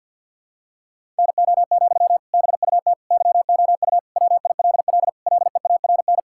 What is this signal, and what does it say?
Morse code, heard as a keyed tone, about 52 wpm, 700 Hz. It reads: NO, BUT YOU WILL LEARN
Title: Sentences from Top 1000 Words - Review Part 6 - 50wpm